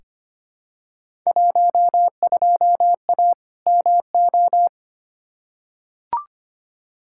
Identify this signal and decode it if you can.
Morse code, heard as a keyed tone, about 25 wpm, 700 Hz. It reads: 12A MO
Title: Field Day Exchanges Encore - 25wpm